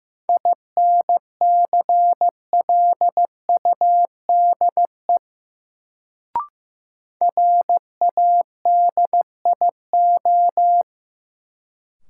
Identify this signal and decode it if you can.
Morse code, heard as a keyed tone, about 15 wpm, 700 Hz. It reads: INCLUDE RADIO
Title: Sentences from Top 1000 Words - Review Part 3 - New Word Drill - 15wpm